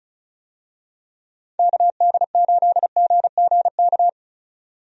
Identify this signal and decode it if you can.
Morse code, heard as a keyed tone, about 34 wpm, 700 Hz. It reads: KD8GGK
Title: USA Call Signs - 6 Characters - 35wpm